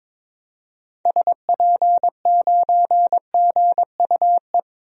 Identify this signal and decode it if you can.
Morse code, heard as a keyed tone, about 22 wpm, 700 Hz. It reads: SP9GUE
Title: DXCC-100 Call Signs - 6 Characters - Repeated 3x - 22wpm